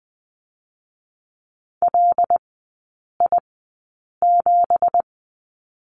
Morse code, heard as a keyed tone, about 20 wpm, spaced as for 8 wpm, 700 Hz. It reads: LI7
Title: 166 Lesson 27e - Number 7 - Three Characters 20wpm